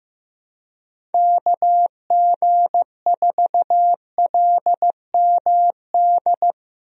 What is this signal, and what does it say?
Morse code, heard as a keyed tone, about 15 wpm, 700 Hz. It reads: KG4LMD
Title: USA Call Signs - 6 Characters - Repeated 2x - 15wpm